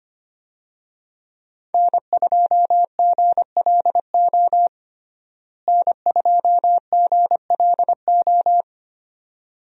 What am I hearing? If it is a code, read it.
Morse code, heard as a keyed tone, about 25 wpm, 700 Hz. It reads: N2GLO N2GLO
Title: USA Call Signs - 5 Characters - Repeated 2x - 25wpm